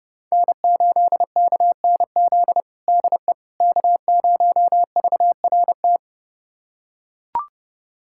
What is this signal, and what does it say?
Morse code, heard as a keyed tone, about 32 wpm, 700 Hz. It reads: N8KNZ DE K0VRT
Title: USA Callsign DE Callsign - 5 Characters - 30wpm